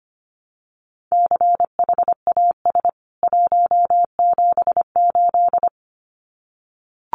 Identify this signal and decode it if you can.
Morse code, heard as a keyed tone, about 25 wpm, 700 Hz. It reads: CHAS 178
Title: CWT Exchange 25wpm